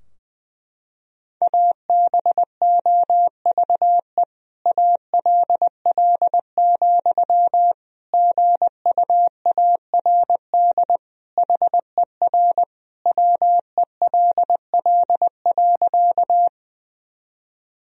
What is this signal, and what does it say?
Morse code, heard as a keyed tone, about 20 wpm, 700 Hz. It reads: ABOVE ALL, GUARD HER WELL.
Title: Sentences from Top 1000 Words - Review Part 4 - 20wpm